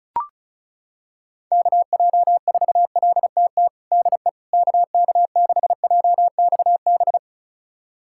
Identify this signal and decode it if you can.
Morse code, heard as a keyed tone, about 34 wpm, 700 Hz. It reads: KJ4LTT DE KK6JXB
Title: USA Callsign DE Callsign 35wpm